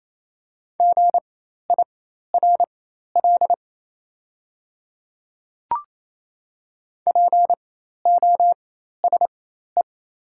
Morse code, heard as a keyed tone, about 27 wpm, spaced as for 13 wpm, 690 Hz. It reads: GIRL POSE
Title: 4x Character Spacing - Top 500 Words - 28wpm